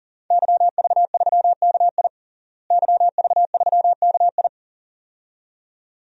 Morse code, heard as a keyed tone, about 37 wpm, 700 Hz. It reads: YV3KI YV3KI
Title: DXCC-100 Call Signs - 5 Characters - Repeated 2x - 40wpm